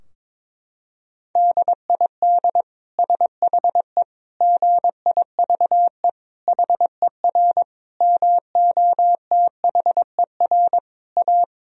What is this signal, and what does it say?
Morse code, heard as a keyed tone, about 22 wpm, 700 Hz. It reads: DID SHE GIVE HER MOTHER A
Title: Sentences from Top 1000 Words - Review Part 4 - 22wpm